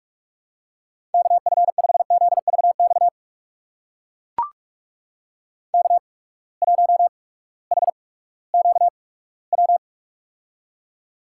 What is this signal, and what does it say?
Morse code, heard as a keyed tone, about 44 wpm, spaced as for 20 wpm, 700 Hz. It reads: KF5ZVX K1HQW
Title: USA Call Signs - 8x Character Spacing - 45wpm